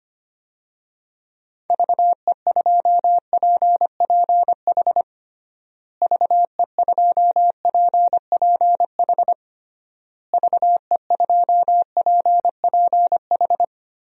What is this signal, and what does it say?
Morse code, heard as a keyed tone, about 25 wpm, 700 Hz. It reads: VE2PPH VE2PPH VE2PPH
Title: DXCC-100 Call Signs - 6 Characters - Repeated 3x - 25wpm